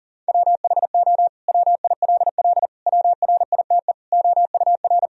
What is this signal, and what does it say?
Morse code, heard as a keyed tone, about 40 wpm, 700 Hz. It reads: WHO WILL WRITE OUR
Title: Sentences from Top 1000 Words - Review Part 3 - 40wpm